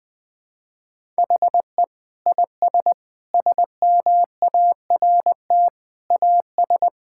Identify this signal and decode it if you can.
Morse code, heard as a keyed tone, about 21 wpm, 700 Hz. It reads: HE IS SMART AS
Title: Sentences with 4-Word Phrases - Spoken Beginning - 20wpm